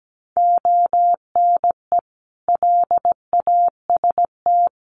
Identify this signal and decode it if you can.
Morse code, heard as a keyed tone, about 17 wpm, 700 Hz. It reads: ONE LAST
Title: Sentences from Top 1200 Words 17wpm